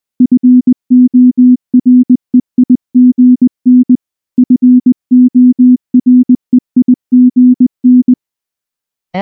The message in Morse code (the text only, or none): FOREIGN FOREIGN